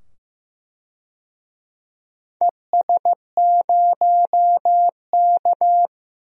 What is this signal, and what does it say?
Morse code, heard as a keyed tone, about 15 wpm, 700 Hz. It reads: ES0K
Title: DXCC-100 Call Signs - 4 Characters - 15wpm